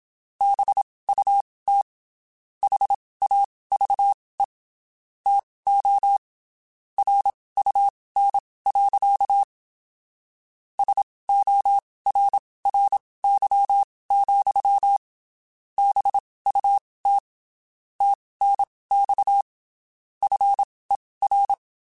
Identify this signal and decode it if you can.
Morse code, heard as a keyed tone, about 27 wpm, spaced as for 19 wpm, 800 Hz. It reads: BUT HAVE TO RUN. SORRY, BUT TNX FER
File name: QSOs Spaced